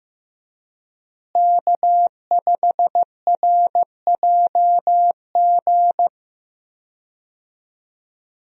Morse code, heard as a keyed tone, about 16 wpm, 700 Hz. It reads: K5RJG
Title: USA Call Signs - 5 Characters - Repeated 3x - 15wpm